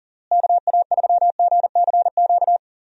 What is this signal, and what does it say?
Morse code, heard as a keyed tone, about 40 wpm, 700 Hz. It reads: KA3GCQ
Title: USA Call Signs - 6 Characters - Repeated 3x - 40wpm